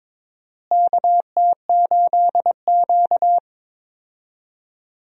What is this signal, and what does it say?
Morse code, heard as a keyed tone, about 22 wpm, 700 Hz. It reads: KT8Q
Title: USA Call Signs - 4 Characters - Repeated 2x - 22wpm